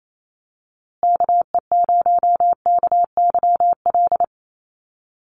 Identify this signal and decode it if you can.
Morse code, heard as a keyed tone, about 28 wpm, 700 Hz. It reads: KE0KYL
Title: USA Call Signs Repeated 3x 28wpm